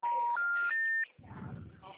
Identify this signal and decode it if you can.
Telephone call progress tone, heard as a telephone network's tone, with noise. Special information tone.